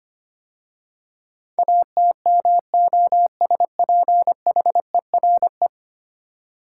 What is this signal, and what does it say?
Morse code, heard as a keyed tone, about 25 wpm, 700 Hz. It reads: ATMOSPHERE
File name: 10 Letter Words - First 3 Letters Given - 25wpm